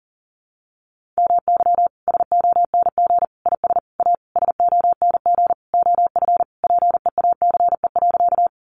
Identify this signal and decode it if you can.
Morse code, heard as a keyed tone, about 40 wpm, 700 Hz. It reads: MY SONG IS A SONG OF PEACE.
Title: Sentences from Top 1500 Words 40wpm